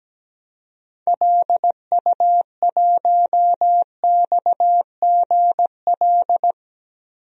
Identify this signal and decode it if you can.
Morse code, heard as a keyed tone, about 17 wpm, 700 Hz. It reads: LU1XGL
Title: DXCC-100 Call Signs - 6 Characters - Repeated 3x - 17wpm